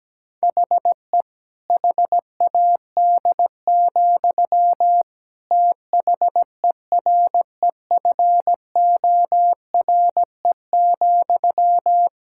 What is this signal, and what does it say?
Morse code, heard as a keyed tone, about 17 wpm, 700 Hz. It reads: HE HAD, THEREFORE,
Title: Sentences from Top 1000 Words - Review Part 5 - 17wpm